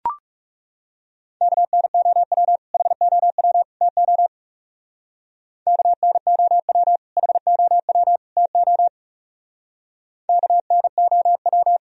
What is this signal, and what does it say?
Morse code, heard as a keyed tone, about 37 wpm, 700 Hz. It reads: KNOW HOW TO KNOW HOW TO KNOW
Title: Speed-Racing - Sets of 3 Words 30wpm